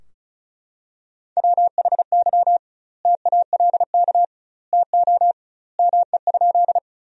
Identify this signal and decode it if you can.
Morse code, heard as a keyed tone, about 36 wpm, 700 Hz. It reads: WHY TALK TO ME?
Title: 2x Word-Spacing - Sentences from Top 400 Words - 35wpm